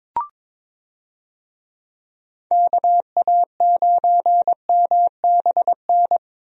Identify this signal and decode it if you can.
Morse code, heard as a keyed tone, about 22 wpm, 700 Hz. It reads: KA9MBN
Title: USA Call Signs - 6 Characters - Repeated 2x - 22wpm